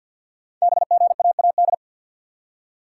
Morse code, heard as a keyed tone, about 48 wpm, 700 Hz. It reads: BGAAD
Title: Code Groups - 5 Letters - 50wpm